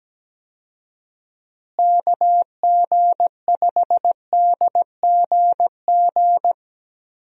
Morse code, heard as a keyed tone, about 17 wpm, 700 Hz. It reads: KG5DGG
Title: USA Call Signs - 6 Characters - Repeated 3x - 17wpm